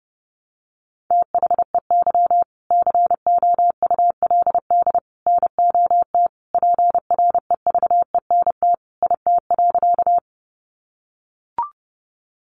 Morse code, heard as a keyed tone, about 30 wpm, 700 Hz. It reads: THEY COULD NOT PREVENT IT.
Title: Sentences from Top 1100 Words 30wpm